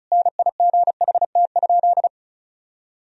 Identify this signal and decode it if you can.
Morse code, heard as a keyed tone, about 34 wpm, 700 Hz. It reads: NIGHT?
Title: Sentences from Top 1000 Words - Review Part 2 - 35wpm